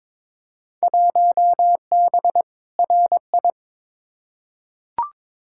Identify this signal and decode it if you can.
Morse code, heard as a keyed tone, about 22 wpm, 700 Hz. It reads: 1B RI
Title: Field Day Exchanges Encore - 22wpm